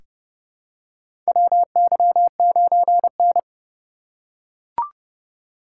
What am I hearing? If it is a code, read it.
Morse code, heard as a keyed tone, about 31 wpm, 700 Hz. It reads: WY9N
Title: USA Call Signs - 4 Characters - 2x Character Spacing - 30wpm